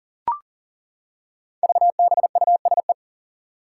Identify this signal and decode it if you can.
Morse code, heard as a keyed tone, about 42 wpm, 700 Hz. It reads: VBUSE
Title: Five Challenging Letter-Number Combinations 40wpm